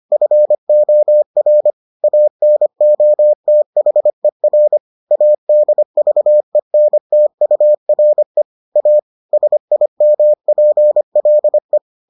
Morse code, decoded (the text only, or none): FOR ANOTHER ADVENTURE A SIMPLE